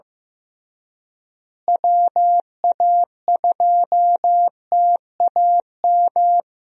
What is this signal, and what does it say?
Morse code, heard as a keyed tone, about 15 wpm, 700 Hz. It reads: WA2TAM
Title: USA Call Signs - 6 Characters - Repeated 3x - 15wpm